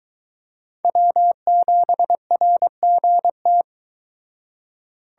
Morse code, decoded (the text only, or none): W7RGT